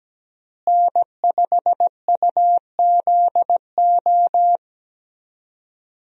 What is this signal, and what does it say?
Morse code, heard as a keyed tone, about 17 wpm, 700 Hz. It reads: N5UZO
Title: USA Call Signs - 5 Characters - Repeated 2x - 17wpm